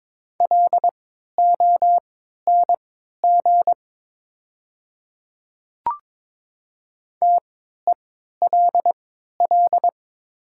Morse code, heard as a keyed tone, about 22 wpm, spaced as for 12 wpm, 700 Hz. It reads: LONG TELL
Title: 3x Character Spacing - Top 200 Words - 22wpm